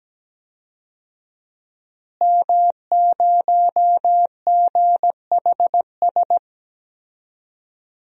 Morse code, heard as a keyed tone, about 17 wpm, 700 Hz. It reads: M0GHS
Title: DXCC-100 Call Signs - 5 Characters - Repeated 2x - 17wpm